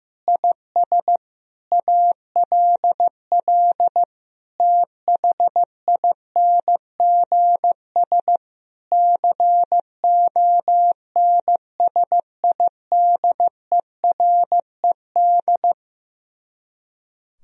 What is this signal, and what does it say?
Morse code, heard as a keyed tone, about 15 wpm, 700 Hz. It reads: IS ALL THINGS CONSIDERED
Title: Speed-Racing - Sets of 5 Words 15wpm